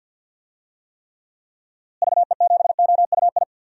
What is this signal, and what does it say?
Morse code, heard as a keyed tone, about 47 wpm, 700 Hz. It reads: VE7OUI
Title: DXCC-100 Call Signs - 6 Characters - Repeated 3x - 50wpm